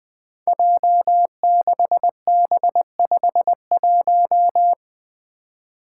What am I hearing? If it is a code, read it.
Morse code, heard as a keyed tone, about 20 wpm, 700 Hz. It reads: J6B51
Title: Five Challenging Letter-Number Combinations 20wpm